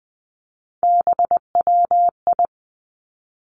Morse code, heard as a keyed tone, about 20 wpm, 700 Hz. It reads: BWI
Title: Three Challenging Letter-Number Combinations 20wpm